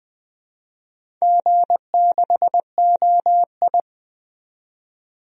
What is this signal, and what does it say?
Morse code, heard as a keyed tone, about 20 wpm, 700 Hz. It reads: G6OI
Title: DXCC-100 Call Signs - 4 Characters - 20wpm